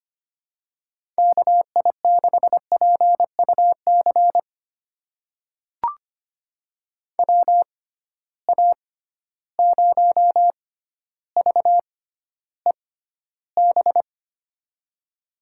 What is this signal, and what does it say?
Morse code, heard as a keyed tone, about 25 wpm, spaced as for 13 wpm, 700 Hz. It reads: KI6PUC WA0VEB
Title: USA Call Signs - 6 Characters - 6x Character Spacing - 25wpm